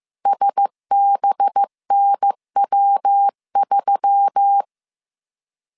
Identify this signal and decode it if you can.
Morse code, heard as a keyed tone, about 15 wpm, 800 Hz. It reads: SBNW3